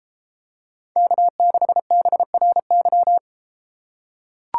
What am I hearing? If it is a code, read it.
Morse code, heard as a keyed tone, about 31 wpm, 700 Hz. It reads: K6BRY